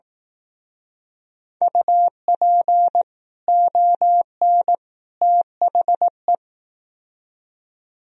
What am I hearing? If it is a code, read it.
Morse code, heard as a keyed tone, about 18 wpm, 700 Hz. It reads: UP ON THE
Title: Speed-Racing - Sets of 3 Words 15wpm